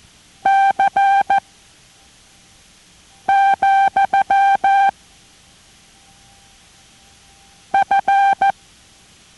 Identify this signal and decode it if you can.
Morse code, heard as a keyed tone, about 14 wpm, 790 Hz. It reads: C , F